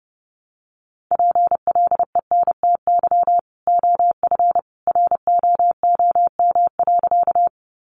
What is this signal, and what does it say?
Morse code, heard as a keyed tone, about 30 wpm, 700 Hz. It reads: PLENTY OF ROOM.
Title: Sentences from Top 1300 Words 30wpm